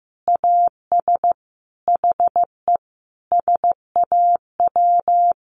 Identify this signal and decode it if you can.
Morse code, heard as a keyed tone, about 15 wpm, 700 Hz. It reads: AS HE SAW
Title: Sentences from Top 1100 Words 15wpm